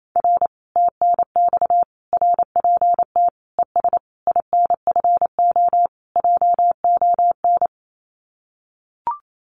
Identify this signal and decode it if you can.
Morse code, heard as a keyed tone, about 28 wpm, 700 Hz. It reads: R TNX RPT ES INFO JON
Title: Common QSO Phrases Encore 28wpm